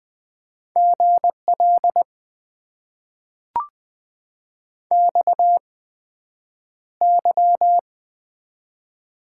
Morse code, heard as a keyed tone, about 20 wpm, spaced as for 9 wpm, 700 Hz. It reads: GL XY
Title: Top 90 QSO Elements - 8x Character Spacing - 20wpm